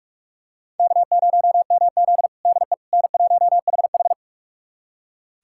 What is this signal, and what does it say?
Morse code, heard as a keyed tone, about 44 wpm, 690 Hz. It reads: K0MZ DE N1HH
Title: USA Callsign DE Callsign - 4 Characters - 45wpm